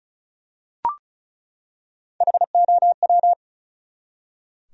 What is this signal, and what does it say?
Morse code, heard as a keyed tone, about 36 wpm, 700 Hz. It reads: HOW
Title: Top 100 Words 35wpm